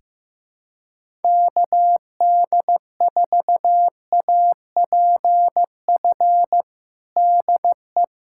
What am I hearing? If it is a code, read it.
Morse code, heard as a keyed tone, about 15 wpm, 700 Hz. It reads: KD4APF DE
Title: USA Callsign DE Callsign - 6 Characters - 15wpm